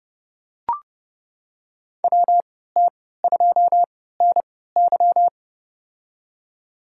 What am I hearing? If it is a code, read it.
Morse code, heard as a keyed tone, about 30 wpm, spaced as for 17 wpm, 700 Hz. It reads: WT2NY
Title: USA Call Signs - 5 Characters - 3x Character Spacing - 30wpm